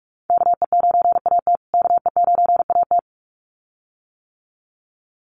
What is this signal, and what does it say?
Morse code, heard as a keyed tone, about 44 wpm, 700 Hz. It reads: KE9AT KE9AT
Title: USA Call Signs Repeated 2x 45wpm